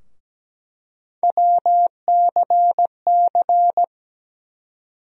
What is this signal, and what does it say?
Morse code, heard as a keyed tone, about 17 wpm, 700 Hz. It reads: WCC
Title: USA Call Sign Postfixes - 3 Letters 17wpm